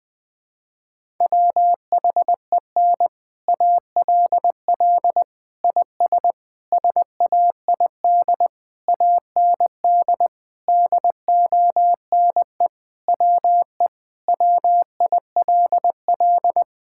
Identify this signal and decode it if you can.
Morse code, heard as a keyed tone, about 20 wpm, 700 Hz. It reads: WHEN ALL IS SAID AND DONE WE WILL
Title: Sentences with 6-Word Phrases - 20wpm